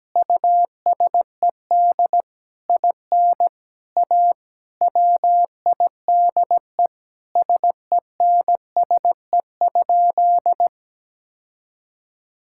Morse code, heard as a keyed tone, about 17 wpm, 700 Hz. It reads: USED IN A WIDE SENSE?
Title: Sentences from Top 1000 Words - Review Part 6 - 17wpm